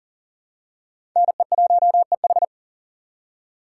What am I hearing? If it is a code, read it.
Morse code, heard as a keyed tone, about 40 wpm, 700 Hz. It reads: NE1EH
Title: USA Call Signs - 3x Character Spacing - 40wpm